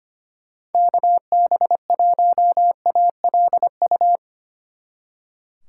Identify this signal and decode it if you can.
Morse code, heard as a keyed tone, about 32 wpm, 700 Hz. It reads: KB1ALU